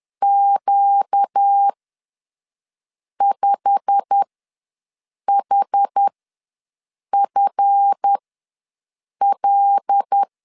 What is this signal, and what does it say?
Morse code, heard as a keyed tone, about 10 wpm, spaced as for 5 wpm, 800 Hz. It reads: Q5HFL
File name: LEZIONI / 10 SIGNS / 10A SLASH ws 10wpm 4121